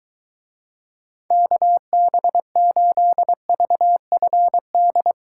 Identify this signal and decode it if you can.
Morse code, heard as a keyed tone, about 23 wpm, 700 Hz. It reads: KB8VFD